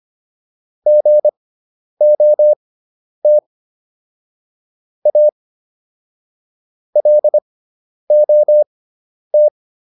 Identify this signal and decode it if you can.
Morse code, heard as a keyed tone, about 25 wpm, spaced as for 10 wpm, 600 Hz. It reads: GOT A LOT